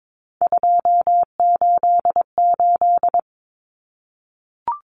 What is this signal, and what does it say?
Morse code, heard as a keyed tone, about 22 wpm, 700 Hz. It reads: 288
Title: Common Cut Numbers - 3-Digit 22wpm